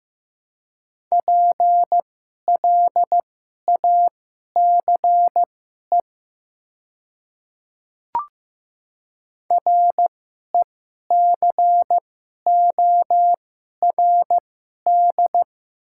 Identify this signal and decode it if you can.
Morse code, heard as a keyed tone, about 15 wpm, spaced as for 11 wpm, 700 Hz. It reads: PLACE RECORD
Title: 2x Character Spacing - Top 500 Words - 15wpm